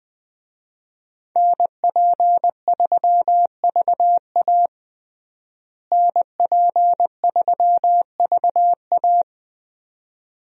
Morse code, heard as a keyed tone, about 21 wpm, 700 Hz. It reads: NP3VA NP3VA
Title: USA Call Signs - 5 Characters - Repeated 2x - 20wpm